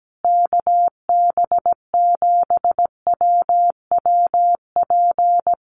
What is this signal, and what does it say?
Morse code, heard as a keyed tone, about 17 wpm, 700 Hz. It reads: KB7WWP
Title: Common QSO Phrases Encore 17wpm